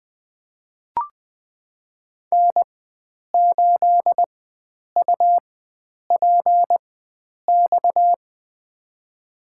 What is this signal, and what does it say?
Morse code, heard as a keyed tone, about 21 wpm, spaced as for 9 wpm, 700 Hz. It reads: N8UPX
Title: USA Call Signs - 5 Characters - 4x Character Spacing - 20wpm